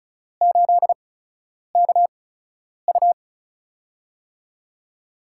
Morse code, heard as a keyed tone, about 36 wpm, spaced as for 10 wpm, 700 Hz. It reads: 8KU
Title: USA Call Signs - 5 Characters - 8x Character Spacing - 35wpm